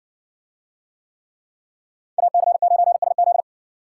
Morse code, heard as a keyed tone, about 60 wpm, 700 Hz. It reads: AC9SZ